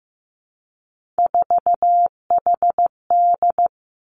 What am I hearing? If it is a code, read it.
Morse code, heard as a keyed tone, about 15 wpm, 700 Hz. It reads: 4HD
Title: Three Challenging Letter-Number Combinations 15wpm